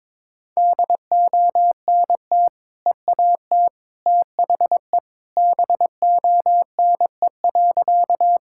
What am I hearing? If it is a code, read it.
Morse code, heard as a keyed tone, about 22 wpm, 700 Hz. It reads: DONT EAT THE BONE.
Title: Sentences from Top 1600 Words 22wpm